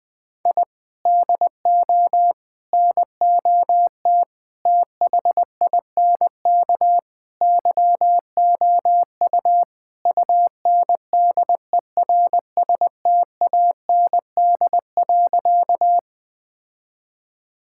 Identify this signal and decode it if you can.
Morse code, heard as a keyed tone, about 20 wpm, 700 Hz. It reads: I DO NOT THINK YOU UNDERSTAND.
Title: Sentences with 5-Word Phrases - 20wpm